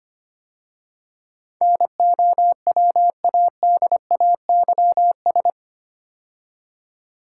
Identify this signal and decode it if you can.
Morse code, heard as a keyed tone, about 25 wpm, 700 Hz. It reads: NOWADAYS
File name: 8 Letter Words - First 3 Letters Given - 25wpm